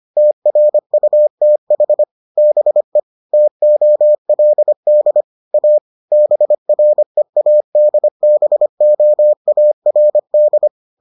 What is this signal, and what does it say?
Morse code, heard as a keyed tone, about 25 wpm, 600 Hz. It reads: TRUTH BE TOLD A BREADBOARD